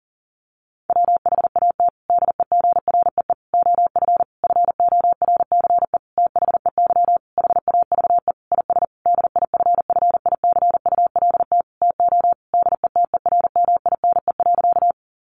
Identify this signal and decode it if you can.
Morse code, heard as a keyed tone, about 40 wpm, 700 Hz. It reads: WHAT DEGREE OF FORCE THEY HAVE IS DIFFICULT TO DETERMINE.
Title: Sentences from Top 1500 Words 40wpm